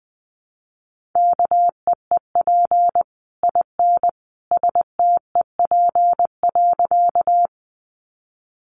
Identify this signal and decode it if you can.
Morse code, heard as a keyed tone, about 20 wpm, 700 Hz. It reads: KEEP IN STEP.
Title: Sentences from Top 400 Words 20wpm